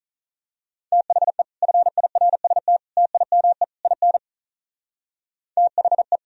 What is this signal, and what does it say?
Morse code, heard as a keyed tone, about 39 wpm, 700 Hz. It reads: THE FIRST TIME IN THE
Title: Speed-Racing - Sets of 4 Words 35wpm